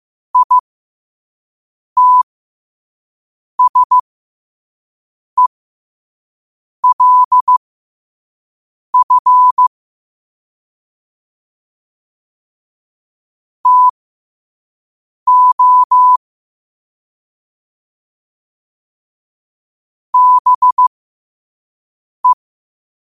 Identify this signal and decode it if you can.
Morse code, heard as a keyed tone, about 15 wpm, spaced as for 5 wpm, 1000 Hz. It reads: ITSELF TO BE